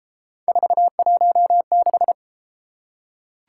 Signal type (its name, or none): Morse code